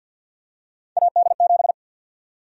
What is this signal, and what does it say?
Morse code, heard as a keyed tone, about 52 wpm, 700 Hz. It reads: AD7
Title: USA Call Sign Prefixes - 50wpm